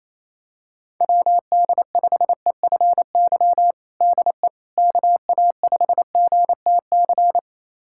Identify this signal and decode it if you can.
Morse code, heard as a keyed tone, about 28 wpm, 700 Hz. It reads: WD5EFY DE KA5GTC
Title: USA Callsign DE Callsign - 6 Characters - 28wpm